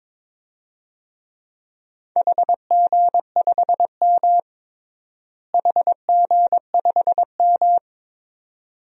Morse code, heard as a keyed tone, about 22 wpm, 700 Hz. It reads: HG5M HG5M
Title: DXCC-100 Call Signs - 4 Characters - Repeated 2x - 22wpm